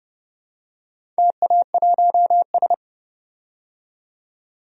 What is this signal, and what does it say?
Morse code, heard as a keyed tone, about 30 wpm, 700 Hz. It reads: TA1S
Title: DXCC-100 Call Signs - 4 Characters - 30wpm